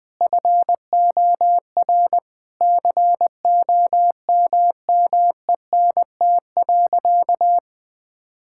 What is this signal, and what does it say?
Morse code, heard as a keyed tone, about 21 wpm, 700 Hz. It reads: FOR COMMENT.
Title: Sentences with 6-Word Phrases - 20wpm